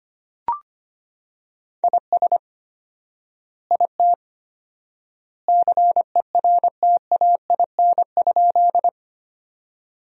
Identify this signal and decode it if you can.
Morse code, heard as a keyed tone, about 25 wpm, 700 Hz. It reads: IS IT CERTAIN?
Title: 4x Word-Spacing - Sentences from Top 500 Words - 25wpm